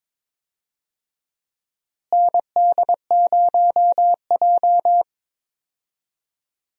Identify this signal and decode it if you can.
Morse code, heard as a keyed tone, about 22 wpm, 700 Hz. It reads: ND0J
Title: USA Call Signs - 4 Characters - Repeated 2x - 22wpm